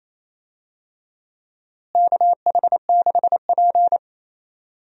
Morse code, heard as a keyed tone, about 28 wpm, 700 Hz. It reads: KH6P
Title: DXCC-100 Call Signs - 4 Characters - Repeated 3x - 28wpm